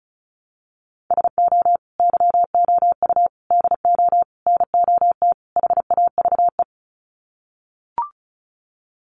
Morse code, heard as a keyed tone, about 34 wpm, 700 Hz. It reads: SO YOU DO NOT HAVE
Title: Sets of 5 Words Encore 35wpm